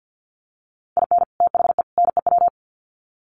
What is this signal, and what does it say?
Morse code, heard as a keyed tone, about 50 wpm, 700 Hz. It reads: IN THE NEW
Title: Sets of 3 Words 50wpm